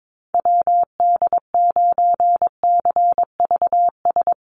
Morse code, heard as a keyed tone, about 22 wpm, 700 Hz. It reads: WD9CVS
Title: USA Call Signs 22wpm